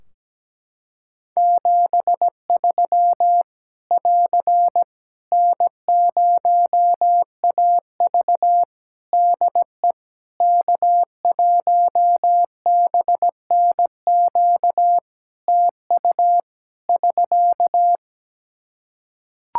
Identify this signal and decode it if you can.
Morse code, heard as a keyed tone, about 17 wpm, 700 Hz. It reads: 73 <AR> N0AV DE K1BNQ TU <SK>
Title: Practice QSO - Exchange 4 - Ending - 17wpm